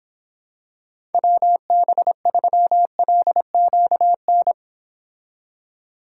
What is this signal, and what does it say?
Morse code, heard as a keyed tone, about 26 wpm, 700 Hz. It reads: WB3LQN